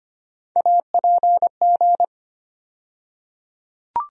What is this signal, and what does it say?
Morse code, heard as a keyed tone, about 25 wpm, 700 Hz. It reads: APG
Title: USA Call Sign Postfixes - 3 Letters 25wpm